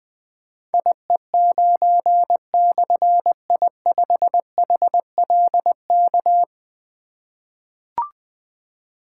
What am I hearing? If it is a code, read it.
Morse code, heard as a keyed tone, about 20 wpm, 700 Hz. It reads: IE9/I5HLK
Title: Difficult Call Signs 20wpm